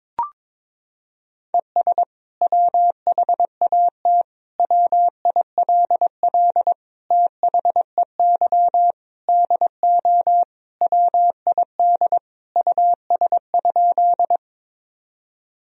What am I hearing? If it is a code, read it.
Morse code, heard as a keyed tone, about 22 wpm, 700 Hz. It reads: ES WHAT WILL THEY DO WID US?
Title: Sentences from Top 200 Words and Abbreviations 22wpm